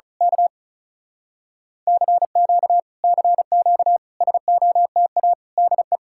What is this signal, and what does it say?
Morse code, heard as a keyed tone, about 32 wpm, 700 Hz. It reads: K CQ CQ SOTA DE
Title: Speed-Racing Common QSO Phrases - 35wpm